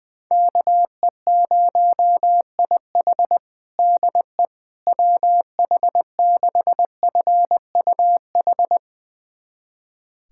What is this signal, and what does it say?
Morse code, heard as a keyed tone, about 20 wpm, 700 Hz. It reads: KE0IH DE WH6FUH
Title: USA Callsign DE Callsign 20wpm